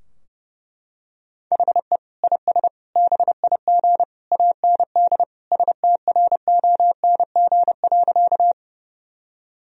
Morse code, heard as a keyed tone, about 30 wpm, 700 Hz. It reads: HE IS BIG AND STRONG.
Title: Sentences from Top 1000 Words - Review Part 3 - 30wpm